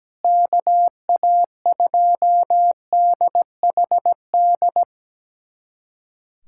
Morse code, heard as a keyed tone, about 17 wpm, 700 Hz. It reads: KA2DHD
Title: USA Callsign DE Callsign - 6 Characters - 17wpm